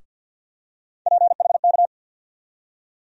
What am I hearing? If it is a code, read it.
Morse code, heard as a keyed tone, about 52 wpm, 700 Hz. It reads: PHK
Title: USA Call Sign Postfixes - 3 Letters 50wpm